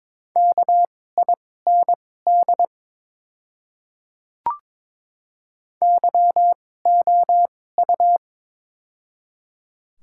Morse code, heard as a keyed tone, about 22 wpm, spaced as for 16 wpm, 700 Hz. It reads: KIND YOU
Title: 2x Character Spacing - Top 500 Words - 22wpm